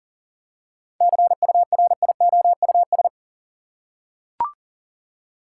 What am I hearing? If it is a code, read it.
Morse code, heard as a keyed tone, about 40 wpm, 700 Hz. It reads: CURIOUS
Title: Seven Letter Words - 40wpm